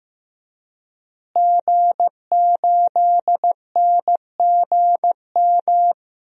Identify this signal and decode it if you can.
Morse code, heard as a keyed tone, about 15 wpm, 700 Hz. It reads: G8NGM
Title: DXCC-100 Call Signs - 5 Characters - Repeated 3x - 15wpm